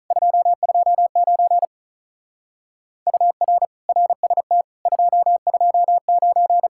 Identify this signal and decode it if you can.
Morse code, heard as a keyed tone, about 43 wpm, 700 Hz. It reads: 229 UR RST 229
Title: Speed-Racing Common QSO Phrases - 35wpm